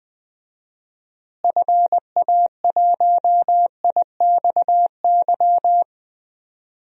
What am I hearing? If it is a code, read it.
Morse code, heard as a keyed tone, about 20 wpm, 700 Hz. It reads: FA1IXY
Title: DXCC-100 Call Signs - 6 Characters - Repeated 2x - 20wpm